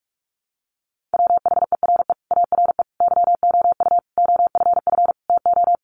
Morse code, heard as a keyed tone, about 44 wpm, 700 Hz. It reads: WHERE ARE YOU OFF TO
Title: Sentences from Top 400 Words 45wpm